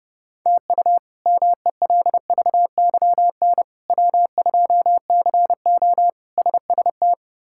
Morse code, heard as a keyed tone, about 30 wpm, 700 Hz. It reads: TU MELVYN W2CO SST
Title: K1USN SST QSO Practice - 30wpm